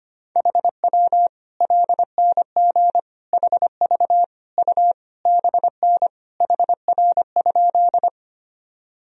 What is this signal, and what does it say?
Morse code, heard as a keyed tone, about 25 wpm, 700 Hz. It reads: HW LNG HV U BN HR?
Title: Sentences from Top 200 Words and Abbreviations 25wpm